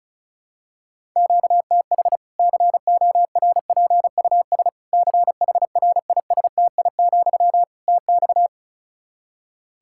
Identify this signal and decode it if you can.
Morse code, heard as a keyed tone, about 36 wpm, 700 Hz. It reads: QTH CORPUS CHRISTI, TX
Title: Practice QSO - Exchange 1 - Beginning - 35wpm